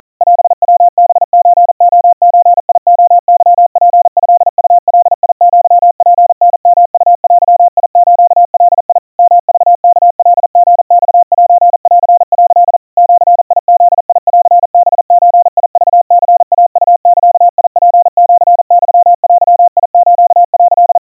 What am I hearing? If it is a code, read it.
Morse code (Japanese Wabun code), heard as a keyed tone, about 40 wpm, 700 Hz. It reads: オヤハソレゾレケツトウガアツタヨウデスガ ヨクワカリマセン」 ジブンハゾクニイウザツシユデス」
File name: kawa / Morse sample / kamyu 150 cpm 700 Hz Crispy